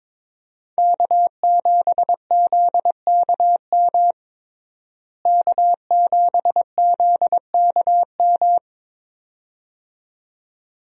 Morse code, heard as a keyed tone, about 22 wpm, 700 Hz. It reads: K7ZKM K7ZKM
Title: USA Call Signs - 5 Characters - Repeated 2x - 22wpm